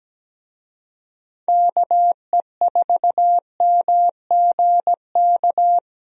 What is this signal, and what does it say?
Morse code, heard as a keyed tone, about 17 wpm, 700 Hz. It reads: KE4MGK
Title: USA Call Signs - 6 Characters - 17wpm